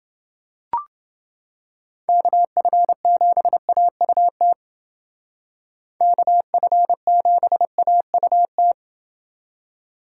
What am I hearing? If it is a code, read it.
Morse code, heard as a keyed tone, about 29 wpm, 700 Hz. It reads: KF7AUT KF7AUT